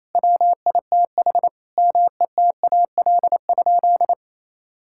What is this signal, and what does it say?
Morse code, heard as a keyed tone, about 28 wpm, 700 Hz. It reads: WITH METAL?
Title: Sentences from Top 1600 Words 28wpm